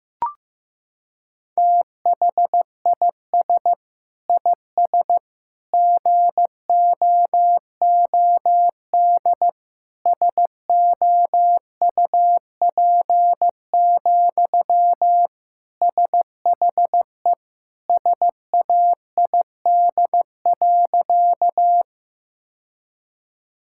Morse code, heard as a keyed tone, about 15 wpm, 700 Hz. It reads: THIS IS GOOD SOUP, SHE SAID.
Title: Sentences from Top 1000 Words - Review Part 6 - 15wpm